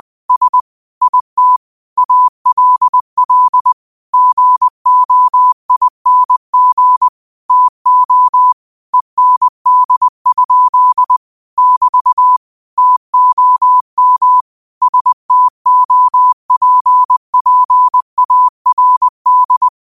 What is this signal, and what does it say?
Morse code, heard as a keyed tone, about 20 wpm, 1000 Hz. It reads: S IT ALL GOING TO END? <BT> TOM STOPPARD